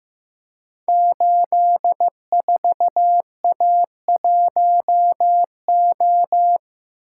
Morse code, heard as a keyed tone, about 15 wpm, 700 Hz. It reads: 84A1O
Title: Code Groups - 5 Letter:Numbers - 15wpm